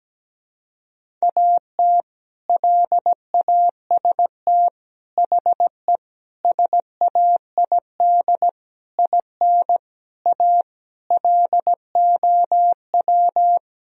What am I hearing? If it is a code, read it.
Morse code, heard as a keyed tone, about 17 wpm, 700 Hz. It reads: AT LAST HE SAID IN A LOW
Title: Sentences from Top 1000 Words - Review Part 1 - 17wpm